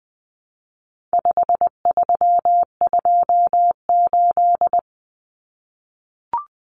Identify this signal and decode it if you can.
Morse code, heard as a keyed tone, about 20 wpm, 700 Hz. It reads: 5328
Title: Numbers - 4-Digit 20wpm